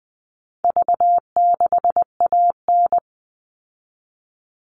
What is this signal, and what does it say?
Morse code, heard as a keyed tone, about 20 wpm, 700 Hz. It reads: V6AN
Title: All Cut Numbers - 4-Digit 20wpm